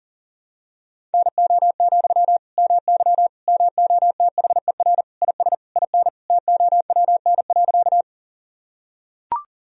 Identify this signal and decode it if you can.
Morse code, heard as a keyed tone, about 40 wpm, 700 Hz. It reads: NO, MY MOTHER IS IN TOWN.
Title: Sentences from Top 1000 Words - Review Part 2 - 40wpm